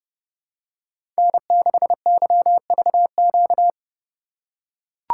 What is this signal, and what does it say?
Morse code, heard as a keyed tone, about 30 wpm, 700 Hz. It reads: N6YVQ
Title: USA Call Signs - 5 Characters - 4x Character Spacing - 30wpm